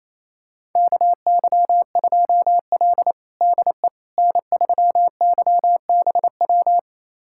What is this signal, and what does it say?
Morse code, heard as a keyed tone, about 28 wpm, 700 Hz. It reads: KY2L DE N3YBW
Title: USA Callsign DE Callsign 28wpm